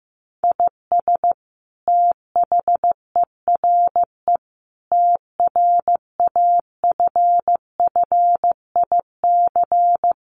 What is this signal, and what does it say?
Morse code, heard as a keyed tone, about 15 wpm, 700 Hz. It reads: IS THERE TRAFFIC
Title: Sentences from Top 1300 Words 15wpm